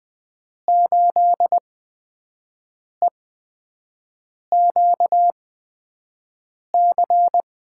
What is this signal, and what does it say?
Morse code, heard as a keyed tone, about 20 wpm, spaced as for 5 wpm, 700 Hz. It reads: 8EQC
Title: USA Call Signs - 5 Characters - 8x Character Spacing - 20wpm